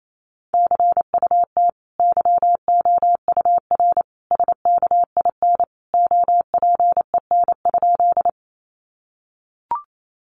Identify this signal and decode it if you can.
Morse code, heard as a keyed tone, about 28 wpm, 700 Hz. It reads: CUT YOUR SKIN OPEN?
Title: Sentences from Top 1300 Words 28wpm